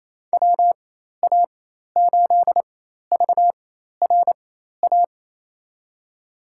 Morse code, heard as a keyed tone, about 28 wpm, spaced as for 13 wpm, 700 Hz. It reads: WA8VRA
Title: USA Call Signs - 6 Characters - 4x Character Spacing - 28wpm